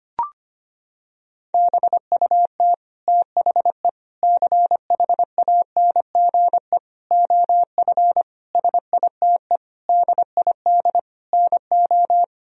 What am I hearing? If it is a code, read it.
Morse code, heard as a keyed tone, about 25 wpm, 700 Hz. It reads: BUT THE CHANGE OF SITE DID NO
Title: Sentences from Top 1000 Words - Review Part 4 - 25wpm